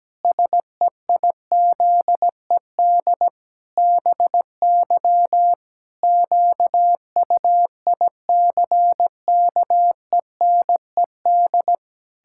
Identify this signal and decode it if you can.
Morse code, heard as a keyed tone, about 17 wpm, 700 Hz. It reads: SEIZED BY QUICKENED
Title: Pangram Challenge v2 - 17wpm